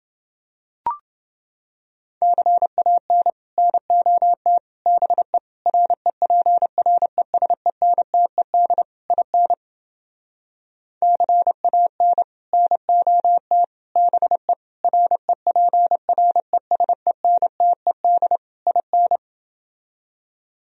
Morse code, heard as a keyed tone, about 30 wpm, 700 Hz. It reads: CAN NOT BE REPRESENTED IN CAN NOT BE REPRESENTED IN